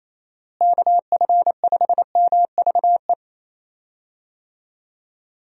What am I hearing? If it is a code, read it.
Morse code, heard as a keyed tone, about 28 wpm, 700 Hz. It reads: KF5MVE
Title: USA Call Signs - 6 Characters - Repeated 3x - 28wpm